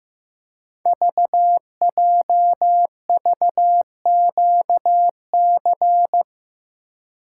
Morse code, heard as a keyed tone, about 15 wpm, 700 Hz. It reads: VJVQC
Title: Code Groups - 5 Letters - 15wpm